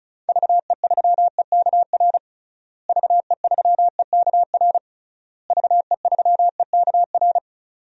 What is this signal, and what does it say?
Morse code, heard as a keyed tone, about 36 wpm, 700 Hz. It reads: VE3EKR VE3EKR VE3EKR
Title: DXCC-100 Call Signs - 6 Characters - Repeated 3x - 35wpm